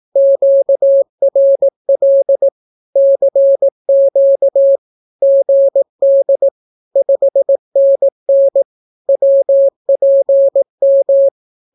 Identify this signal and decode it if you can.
Morse code, heard as a keyed tone, about 18 wpm, 550 Hz. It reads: QRL CQ GD 5NN WPM
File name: naukapodstaw / skroty / skroty